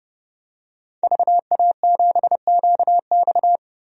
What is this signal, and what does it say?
Morse code, heard as a keyed tone, about 30 wpm, 700 Hz. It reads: VA7QX
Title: DXCC-100 Call Signs - 5 Characters - 30wpm